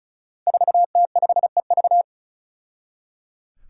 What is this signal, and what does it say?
Morse code, heard as a keyed tone, about 34 wpm, 700 Hz. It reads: 4T5EV
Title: Five Challenging Letter-Number Combinations 35wpm